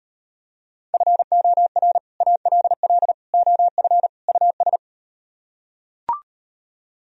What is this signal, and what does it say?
Morse code, heard as a keyed tone, about 39 wpm, 700 Hz. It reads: FOR ALL OF US